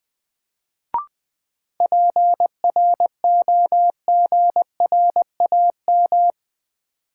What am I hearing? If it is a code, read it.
Morse code, heard as a keyed tone, about 20 wpm, 700 Hz. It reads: PROGRAM
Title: Sentences from Top 1500 Words - New Word Drill Alt - 20wpm